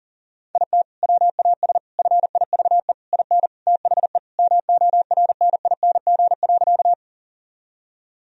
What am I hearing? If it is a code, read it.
Morse code, heard as a keyed tone, about 40 wpm, 700 Hz. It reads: IT WAS FIVE IN THE MORNING.
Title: Sentences from Top 1000 Words - Review Part 2 - 40wpm